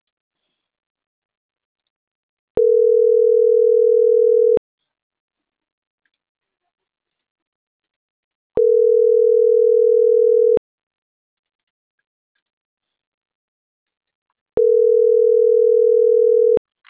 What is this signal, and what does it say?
Ringback tone (North America), 440 + 480 Hz.